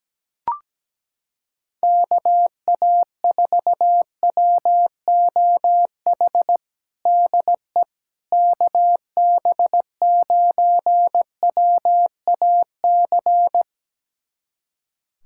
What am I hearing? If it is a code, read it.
Morse code, heard as a keyed tone, about 17 wpm, 700 Hz. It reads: KA4WOH DE KB9WAC
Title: USA Callsign DE Callsign 17wpm